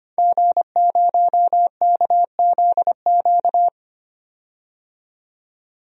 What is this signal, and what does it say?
Morse code, heard as a keyed tone, about 25 wpm, 700 Hz. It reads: G0KZQ
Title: DXCC-100 Call Signs - 5 Characters - 25wpm